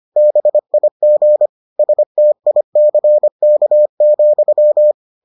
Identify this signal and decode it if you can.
Morse code, heard as a keyed tone, about 25 wpm, 600 Hz. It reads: BIG STICK,